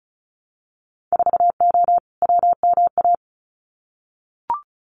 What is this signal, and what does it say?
Morse code, heard as a keyed tone, about 36 wpm, 700 Hz. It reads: 4O WMA
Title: Winter Field Day Exchanges Encore 35wpm